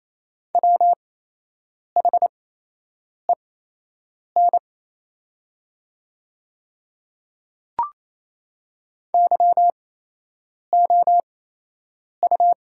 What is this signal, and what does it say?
Morse code, heard as a keyed tone, about 28 wpm, spaced as for 8 wpm, 700 Hz. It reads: WHEN YOU
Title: 8x Character Spacing - Top 200 Words - 28wpm